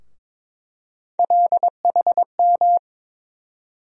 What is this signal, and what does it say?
Morse code, heard as a keyed tone, about 22 wpm, 700 Hz. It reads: LHM
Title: USA Call Sign Postfixes - 3 Letters 22wpm